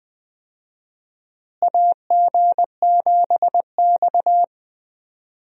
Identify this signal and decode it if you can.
Morse code, heard as a keyed tone, about 21 wpm, 700 Hz. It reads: AG7X
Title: USA Call Signs - 4 Characters - Repeated 3x - 20wpm